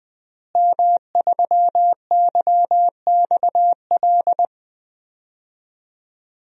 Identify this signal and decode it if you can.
Morse code, heard as a keyed tone, about 20 wpm, 700 Hz. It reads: M3YXL
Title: DXCC-100 Call Signs - 5 Characters - Repeated 2x - 20wpm